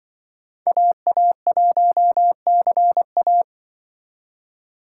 Morse code, heard as a keyed tone, about 24 wpm, 700 Hz. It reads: AA1CA
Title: Speed Racing - USA 2x2 Call Signs 20wpm